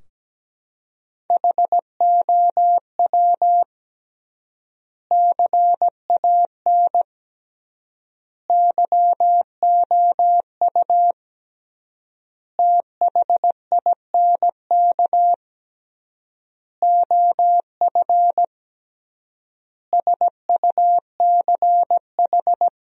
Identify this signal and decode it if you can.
Morse code, heard as a keyed tone, about 17 wpm, 700 Hz. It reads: HOW CAN YOU THINK OF SUCH
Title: 3x Word-Spacing - Sentences from Top 200 Words - 17wpm